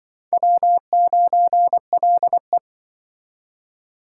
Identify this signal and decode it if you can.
Morse code, heard as a keyed tone, about 23 wpm, 700 Hz. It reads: W9LE
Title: Speed Racing - USA 1x2 Call Signs 20wpm